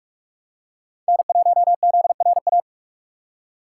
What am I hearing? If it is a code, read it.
Morse code, heard as a keyed tone, about 44 wpm, 700 Hz. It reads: N1ZRA